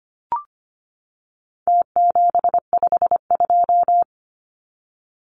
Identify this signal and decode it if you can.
Morse code, heard as a keyed tone, about 25 wpm, 700 Hz. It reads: T752
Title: Numbers - Contest Style - 4-Digit 25wpm